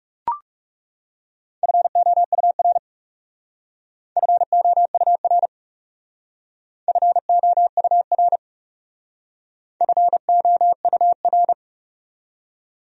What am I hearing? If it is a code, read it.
Morse code, heard as a keyed tone, about 38 wpm, 700 Hz. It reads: FOUR FOUR FOUR FOUR
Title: Speed-Racing Top 300 Words 30wpm